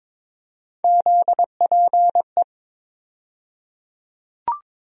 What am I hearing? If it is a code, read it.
Morse code, heard as a keyed tone, about 22 wpm, 700 Hz. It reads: ZPE
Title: USA Call Sign Postfixes - 3 Letters 22wpm